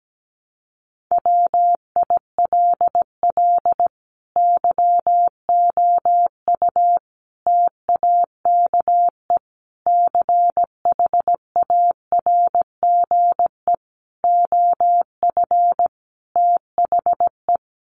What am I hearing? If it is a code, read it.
Morse code, heard as a keyed tone, about 17 wpm, 700 Hz. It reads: WILL YOU TAKE CHARGE OF THE
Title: Sentences from Top 900 Words 17wpm